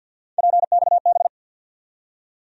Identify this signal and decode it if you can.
Morse code, heard as a keyed tone, about 48 wpm, 700 Hz. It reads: PXB
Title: USA Call Sign Postfixes - 3 Letters 50wpm